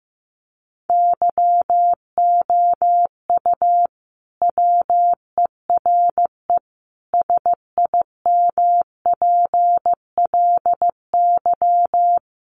Morse code, heard as a keyed tone, about 15 wpm, 700 Hz. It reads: YOU WERE SIMPLY
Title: Sentences from Top 1100 Words 15wpm